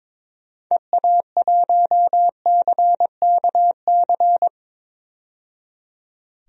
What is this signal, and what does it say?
Morse code, heard as a keyed tone, about 22 wpm, 700 Hz. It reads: EA1CKC
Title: DXCC-100 Call Signs - 6 Characters - 22wpm